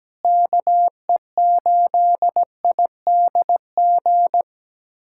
Morse code, heard as a keyed tone, about 17 wpm, 700 Hz. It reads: KE8IDG
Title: USA Callsign DE Callsign 17wpm